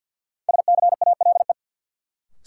Morse code, heard as a keyed tone, about 50 wpm, 700 Hz. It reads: SCALE
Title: Sentences from Top 1000 Words - Review Part 4 - New Word Drill - 50wpm